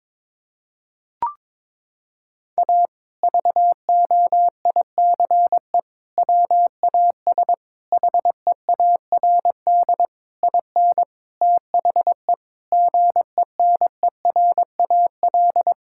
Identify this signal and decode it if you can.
Morse code, heard as a keyed tone, about 21 wpm, 700 Hz. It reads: A VOICE WAS HEARD IN THE GENERAL
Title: Sentences from Top 1000 Words - Review Part 5 - 22wpm